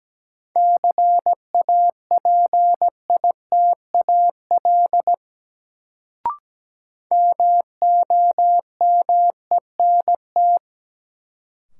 Morse code, heard as a keyed tone, about 17 wpm, 700 Hz. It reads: CAPITAL MOMENT
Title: Sentences from Top 1000 Words - New Word Drill - 17wpm